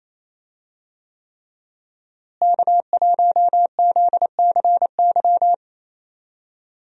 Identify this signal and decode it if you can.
Morse code, heard as a keyed tone, about 28 wpm, 700 Hz. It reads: K1ZCY
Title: USA Call Signs - 5 Characters - Repeated 3x - 28wpm